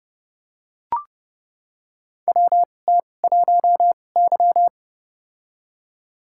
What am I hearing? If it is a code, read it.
Morse code, heard as a keyed tone, about 32 wpm, spaced as for 22 wpm, 700 Hz. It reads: WT1Y
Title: USA Call Signs - 4 Characters - 2x Character Spacing - 30wpm